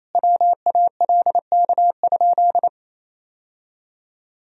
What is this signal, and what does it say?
Morse code, heard as a keyed tone, about 28 wpm, 700 Hz. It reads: WALK?
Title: Sentences from Top 1000 Words - Review Part 1 - 28wpm